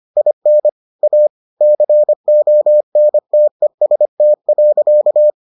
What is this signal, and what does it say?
Morse code, heard as a keyed tone, about 25 wpm, 600 Hz. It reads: IN A CONTEST.